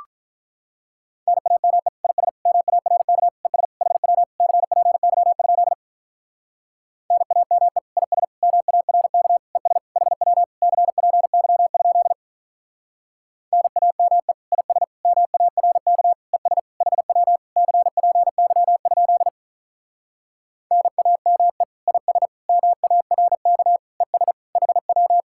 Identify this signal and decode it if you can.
Morse code, heard as a keyed tone, about 45 wpm, 700 Hz. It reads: NAME IS MARK ES HW CPY? NAME IS MARK ES HW CPY? NAME IS MARK ES HW CPY? NAME IS MARK ES HW